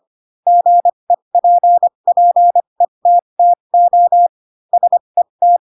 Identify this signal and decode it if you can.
Morse code, heard as a keyed tone, about 25 wpm, spaced as for 22 wpm, 700 Hz. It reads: GEPPETTO SET
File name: Pinocchio / Pinocchio ch3 18 25 ik1hge